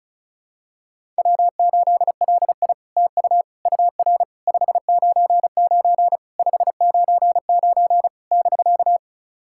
Morse code, heard as a keyed tone, about 36 wpm, 700 Hz. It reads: W8LI TU UR 599 599 <BK>
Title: SOTA QSO Practice 35wpm